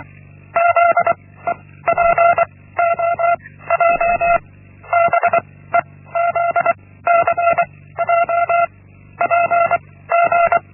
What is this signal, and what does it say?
Morse code, heard as a keyed tone, about 24 wpm, 680 Hz. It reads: Z E P O J B E Z C J P G
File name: Linksmakalnis-1